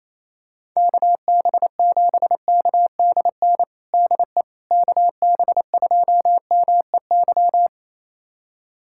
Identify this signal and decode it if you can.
Morse code, heard as a keyed tone, about 28 wpm, 700 Hz. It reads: KB7KDN DE KB2MEY
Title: USA Callsign DE Callsign - 6 Characters - 28wpm